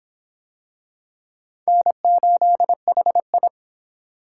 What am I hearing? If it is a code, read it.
Morse code, heard as a keyed tone, about 23 wpm, 700 Hz. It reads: N8HI